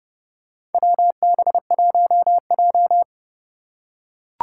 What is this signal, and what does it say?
Morse code, heard as a keyed tone, about 30 wpm, 700 Hz. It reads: WB1J
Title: USA Call Signs - 4 Characters - 4x Character Spacing - 30wpm